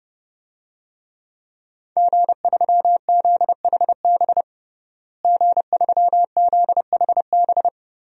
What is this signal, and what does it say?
Morse code, heard as a keyed tone, about 30 wpm, 700 Hz. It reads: G3ZHB G3ZHB
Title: DXCC-100 Call Signs - 5 Characters - Repeated 3x - 30wpm